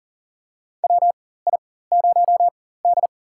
Morse code, heard as a keyed tone, about 40 wpm, spaced as for 19 wpm, 700 Hz. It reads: WI0D
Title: USA Call Signs - 4 Characters - 4x Character Spacing - 40wpm